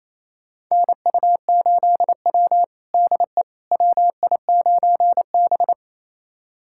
Morse code, heard as a keyed tone, about 28 wpm, 700 Hz. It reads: NU8W DE WI9B
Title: USA Callsign DE Callsign - 4 Characters - 28wpm